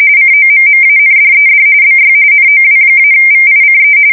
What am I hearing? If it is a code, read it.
RTTY, about 45.45 baud. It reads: COMMANDED BY NOW-VICE